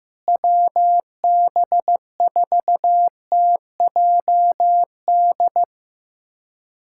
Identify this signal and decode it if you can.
Morse code, heard as a keyed tone, about 15 wpm, 700 Hz. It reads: WB4TJD
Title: USA Callsign DE Callsign 15wpm